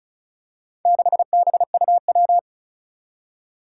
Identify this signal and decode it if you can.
Morse code, heard as a keyed tone, about 36 wpm, 700 Hz. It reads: 6BUW
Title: Four Challenging Letter-Number Combinations 35wpm